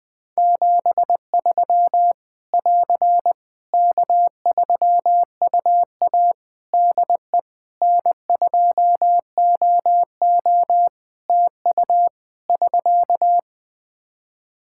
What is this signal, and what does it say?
Morse code, heard as a keyed tone, about 20 wpm, 700 Hz. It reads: 73 <AR> K3UA DE N2OO TU <SK>
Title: K1USN SST QSO Practice 20wpm